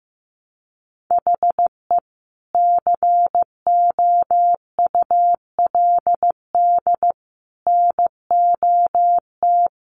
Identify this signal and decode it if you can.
Morse code, heard as a keyed tone, about 15 wpm, 700 Hz. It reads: HE COULD NOT
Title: Sentences from Top 1200 Words 15wpm